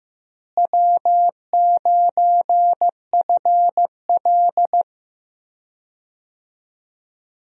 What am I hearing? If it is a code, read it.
Morse code, heard as a keyed tone, about 15 wpm, 700 Hz. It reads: W9FL
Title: USA Call Signs - 4 Characters - Repeated 2x - 15wpm